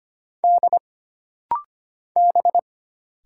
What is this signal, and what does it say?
Morse code, heard as a keyed tone, about 25 wpm, 700 Hz. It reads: D B
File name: Single Character (Letter:Number:Punctuation:Prosign) - Rapid-Fire - 25wpm